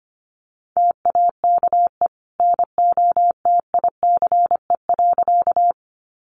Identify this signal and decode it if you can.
Morse code, heard as a keyed tone, about 25 wpm, 700 Hz. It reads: TAKE NOTICE.
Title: Sentences from Top 400 Words 25wpm